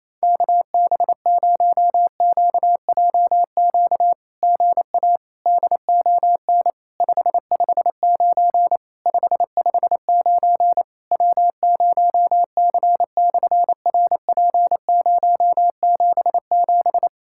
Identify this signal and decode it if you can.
Morse code, heard as a keyed tone, about 28 wpm, 700 Hz. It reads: KB0QJQ GA DON 559 559 W0C/RP077
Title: SOTA QSO Practice 28wpm